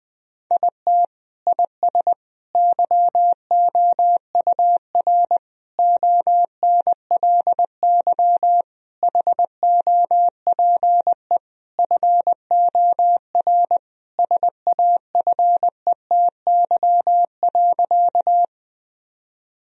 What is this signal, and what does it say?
Morse code, heard as a keyed tone, about 20 wpm, 700 Hz. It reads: IT IS YOUR ONLY HOPE FOR SAFETY.
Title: Sentences from Top 1000 Words - Review Part 4 - 20wpm